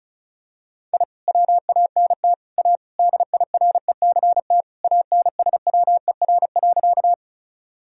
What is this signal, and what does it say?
Morse code, heard as a keyed tone, about 36 wpm, 700 Hz. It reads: I WANT A DIRECT ANSWER.
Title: Sentences from Top 1000 Words - Review Part 2 - 35wpm